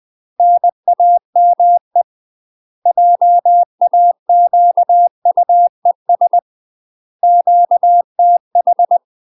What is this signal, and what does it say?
Morse code, heard as a keyed tone, about 20 wpm, 700 Hz. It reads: NAME JAQUES QTH
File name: UK MASTER QSOs TRAINING / 20WPM / UK MASTER QSOs 0001